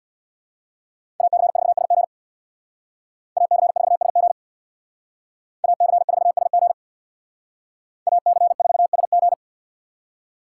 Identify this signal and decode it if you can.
Morse code, heard as a keyed tone, about 64 wpm, 700 Hz. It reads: AC4SG AC4SG AC4SG AC4SG